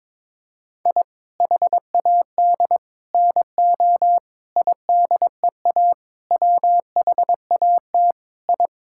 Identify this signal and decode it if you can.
Morse code, heard as a keyed tone, about 22 wpm, 700 Hz. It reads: I HAD NO IDEA WHAT I
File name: practice / 22wpm / Sets of 7 Words - 22wpm